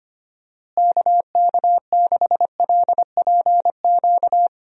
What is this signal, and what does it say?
Morse code, heard as a keyed tone, about 25 wpm, 700 Hz. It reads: KK6LPQ
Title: USA Call Signs - 6 Characters - Repeated 2x - 25wpm